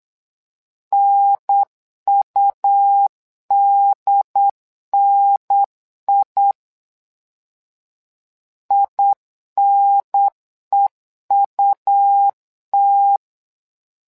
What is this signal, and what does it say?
Morse code, heard as a keyed tone, about 8 wpm, 800 Hz. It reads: NUDNI INEUT
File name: LEZIONI / 2 UHDN / 2 UHDN 8wpm2232